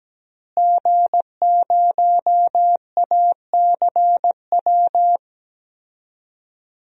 Morse code, heard as a keyed tone, about 17 wpm, 700 Hz. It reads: G0ACW
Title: DXCC-100 Call Signs - 5 Characters - Repeated 3x - 17wpm